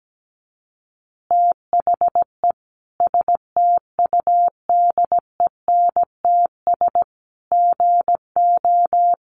Morse code, heard as a keyed tone, about 17 wpm, 700 Hz. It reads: THE STUDENTS GO
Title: Easy Phrases 17wpm